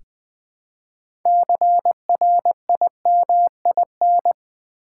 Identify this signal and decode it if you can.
Morse code, heard as a keyed tone, about 20 wpm, 700 Hz. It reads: CRIMIN
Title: Build A Word - 2 Letters At A Time - 8-Letter Words - 20wpm